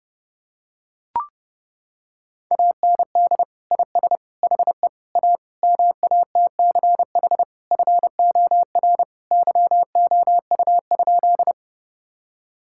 Morse code, heard as a keyed tone, about 30 wpm, 700 Hz. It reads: AND IS HE A MATCH FOR YOU?
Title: Sentences from Top 1600 Words 30wpm